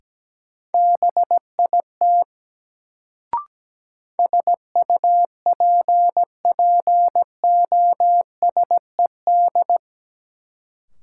Morse code, heard as a keyed tone, about 17 wpm, 700 Hz. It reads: BIT SUPPOSED
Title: Sets of 3 Words - New Word Drill - 17wpm